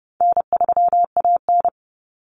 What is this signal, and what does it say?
Morse code, heard as a keyed tone, about 30 wpm, 700 Hz. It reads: N3AN
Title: Common Cut Numbers - 4-Digit 30wpm